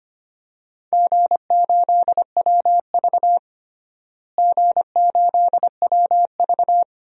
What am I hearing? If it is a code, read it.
Morse code, heard as a keyed tone, about 25 wpm, 700 Hz. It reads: G8WV G8WV
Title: DXCC-100 Call Signs - 4 Characters - Repeated 3x - 25wpm